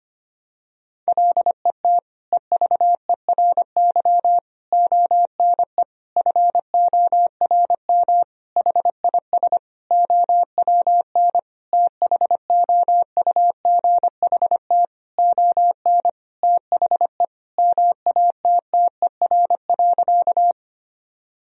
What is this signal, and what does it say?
Morse code, heard as a keyed tone, about 25 wpm, 700 Hz. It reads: LET EVERY ONE FORM HIS OWN THOUGHT ON THE MATTER.
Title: Sentences from Top 1000 Words - Review Part 1 - 25wpm